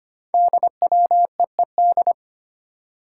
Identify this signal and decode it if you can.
Morse code, heard as a keyed tone, about 25 wpm, 700 Hz. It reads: DWEED
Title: Code Groups - 5 Letter:Numbers - 25wpm